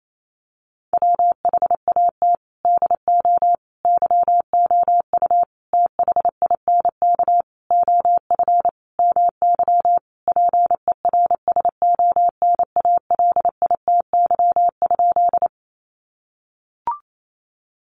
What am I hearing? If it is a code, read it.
Morse code, heard as a keyed tone, about 28 wpm, 700 Hz. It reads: WHAT DO YOU THINK OF MY PERSONALITY?
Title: Sentences from Top 1400 Words 28wpm